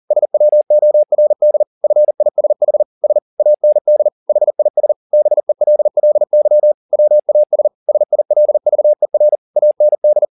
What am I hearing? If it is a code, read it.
Morse code, heard as a keyed tone, about 40 wpm, 600 Hz. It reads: SWORD FISH S AND HIS BELLY WAS SILVER AND
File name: the old man and the sea 2 40wpm